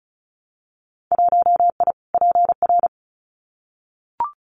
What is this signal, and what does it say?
Morse code, heard as a keyed tone, about 34 wpm, 700 Hz. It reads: 1I PR
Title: Winter Field Day Exchanges Encore 35wpm